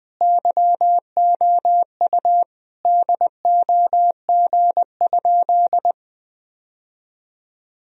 Morse code, heard as a keyed tone, about 20 wpm, 700 Hz. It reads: YOU DOG?
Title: Sentences from Top 1000 Words - Review Part 2 - 20wpm